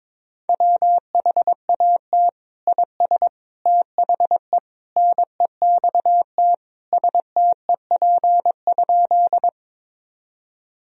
Morse code, heard as a keyed tone, about 22 wpm, 700 Hz. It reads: WHAT IS THE NEXT STEP?
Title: Sentences from Top 1000 Words - Review Part 2 - 22wpm